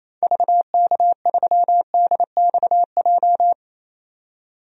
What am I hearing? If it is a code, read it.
Morse code, heard as a keyed tone, about 28 wpm, 700 Hz. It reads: VK3DXJ
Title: DXCC-100 Call Signs - 6 Characters - 28wpm